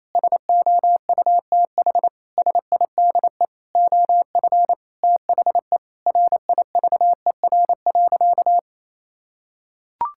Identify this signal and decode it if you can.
Morse code, heard as a keyed tone, about 28 wpm, 700 Hz. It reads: SOUTH SIDE OF THE RIVER.
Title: Sentences from Top 1000 Words - Review Part 5 - 28wpm